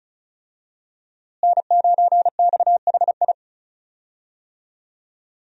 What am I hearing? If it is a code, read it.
Morse code, heard as a keyed tone, about 34 wpm, 700 Hz. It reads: N9XHI
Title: USA Call Signs - 5 Characters - Repeated 3x - 35wpm